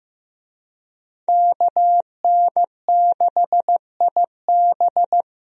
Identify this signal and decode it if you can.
Morse code, heard as a keyed tone, about 15 wpm, 700 Hz. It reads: KN6IB
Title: USA Call Signs - 5 Characters - 15wpm